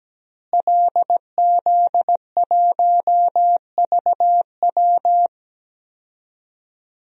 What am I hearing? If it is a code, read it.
Morse code, heard as a keyed tone, about 17 wpm, 700 Hz. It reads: LZ1VW
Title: DXCC-100 Call Signs - 5 Characters - Repeated 3x - 17wpm